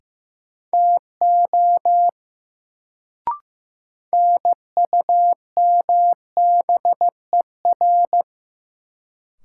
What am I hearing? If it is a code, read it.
Morse code, heard as a keyed tone, about 15 wpm, 700 Hz. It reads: TO NUMBER
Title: Sentences from Top 100 Words - New Word Drill - 15wpm